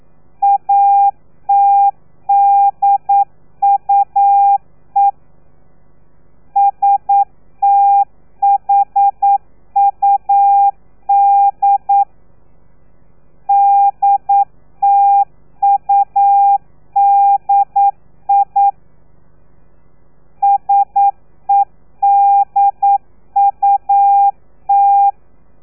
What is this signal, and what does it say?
Morse code, heard as a keyed tone, about 9 wpm, 800 Hz. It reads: ATDUE STHUD DTUDI SEDUT
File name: telegrafia / i2tfj / Lez3 40